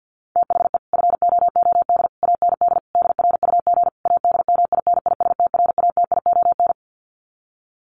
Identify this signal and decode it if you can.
Morse code, heard as a keyed tone, about 50 wpm, 700 Hz. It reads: THE FOOD AND DRUG ADMINISTRATION
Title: Sets of 5 Words 50wpm